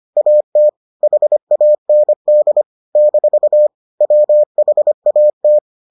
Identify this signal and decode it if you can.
Morse code, heard as a keyed tone, about 25 wpm, 600 Hz. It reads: AT HAND - WHAT